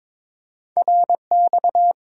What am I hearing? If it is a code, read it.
Morse code, heard as a keyed tone, about 30 wpm, spaced as for 26 wpm, 700 Hz. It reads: RX
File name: Speed-Racing - Two Letter-Number Combinations - 22wpm